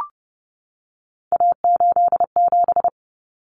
Morse code, heard as a keyed tone, about 30 wpm, 700 Hz. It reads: A87
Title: Common Cut Numbers - 3-Digit 30wpm